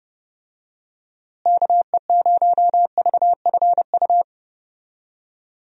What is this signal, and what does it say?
Morse code, heard as a keyed tone, about 30 wpm, 700 Hz. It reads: KE0VFU
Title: USA Call Signs - 6 Characters - Repeated 2x - 30wpm